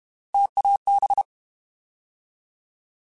Morse code, heard as a keyed tone, about 32 wpm, 800 Hz. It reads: TAB
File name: Alphabetical / Alphabetical mp3s / Ts @30